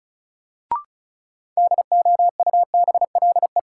Morse code, heard as a keyed tone, about 36 wpm, 700 Hz. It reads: DOUBLE
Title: Sentences from Top 1400 Words - New Word Drill Alt - 35wpm